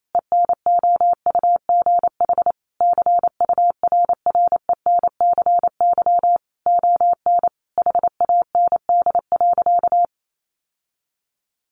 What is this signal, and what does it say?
Morse code, heard as a keyed tone, about 28 wpm, 700 Hz. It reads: ENOUGH CURRENCY ON HAND.
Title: Sentences from Top 1500 Words 28wpm